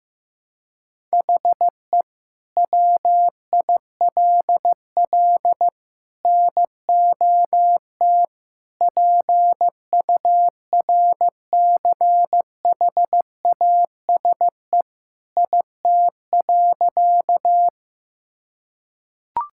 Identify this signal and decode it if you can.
Morse code, heard as a keyed tone, about 15 wpm, 700 Hz. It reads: HE WILL NOT PURCHASE IT.
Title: Sentences from Top 1000 Words - Review Part 5 - 15wpm